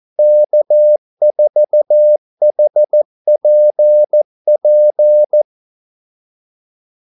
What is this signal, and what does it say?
Morse code, heard as a keyed tone, about 14 wpm, 600 Hz. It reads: K4HPP
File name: Callsigns-14wpm